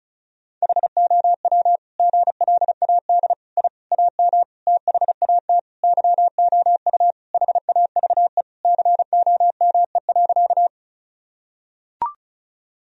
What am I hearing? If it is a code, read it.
Morse code, heard as a keyed tone, about 34 wpm, 700 Hz. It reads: HOW GLAD I AM THAT YOU HAVE COME.
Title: Sentences from Top 1000 Words - Review Part 2 - 35wpm